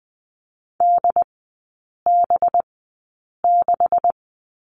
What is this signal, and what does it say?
Morse code, heard as a keyed tone, about 20 wpm, spaced as for 8 wpm, 690 Hz. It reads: DB6
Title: 205 Lesson 33e - Letter j - Three Characters 20wpm